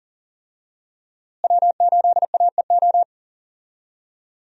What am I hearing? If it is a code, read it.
Morse code, heard as a keyed tone, about 40 wpm, 700 Hz. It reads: W8AEO
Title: USA Call Signs - 5 Characters - Repeated 2x - 40wpm